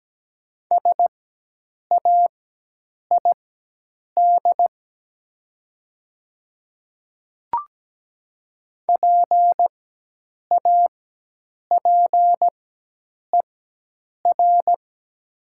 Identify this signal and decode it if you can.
Morse code, heard as a keyed tone, about 17 wpm, spaced as for 8 wpm, 700 Hz. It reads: SAID PAPER
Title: 4x Character Spacing - Top 300 Words - 17wpm